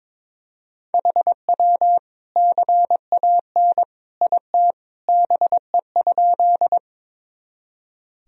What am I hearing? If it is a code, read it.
Morse code, heard as a keyed tone, about 22 wpm, 700 Hz. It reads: HW CAN IT BE?
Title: Sentences from Top 100 Words and Abbreviations 22wpm